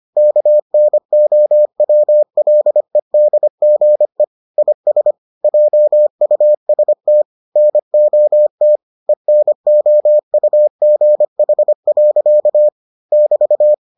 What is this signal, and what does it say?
Morse code, heard as a keyed tone, about 25 wpm, 600 Hz. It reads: KNOWLEDGE IS JUST NOT ENOUGH. <BT>